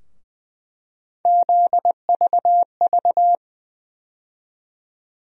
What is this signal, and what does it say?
Morse code, heard as a keyed tone, about 21 wpm, 700 Hz. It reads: ZVV
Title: USA Call Sign Postfixes - 3 Letters 20wpm